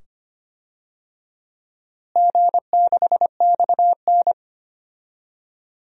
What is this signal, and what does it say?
Morse code, heard as a keyed tone, about 25 wpm, 700 Hz. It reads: G6XN
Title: DXCC-100 Call Signs - 4 Characters - Repeated 2x - 25wpm